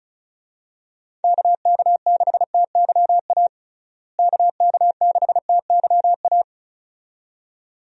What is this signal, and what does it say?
Morse code, heard as a keyed tone, about 36 wpm, 700 Hz. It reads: KK6TYA KK6TYA
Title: USA Call Signs - 6 Characters - Repeated 2x - 35wpm